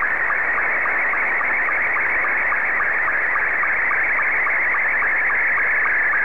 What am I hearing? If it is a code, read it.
SSTV (slow-scan television), Scottie 2.
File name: sstv / S2-VK6AAL